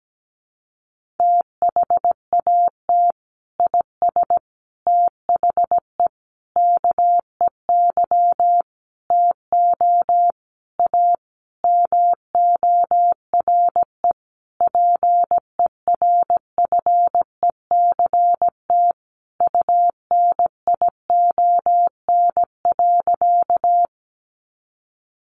Morse code, heard as a keyed tone, about 17 wpm, 700 Hz. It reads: THAT IS THE KEY TO A MORE PERFECT UNION.
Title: Sentences from Top 1400 Words 17wpm